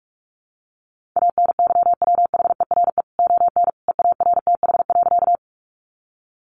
Morse code, heard as a keyed tone, about 44 wpm, 700 Hz. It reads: ANYWHERE ON EARTH.
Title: Sentences from Top 1400 Words 45wpm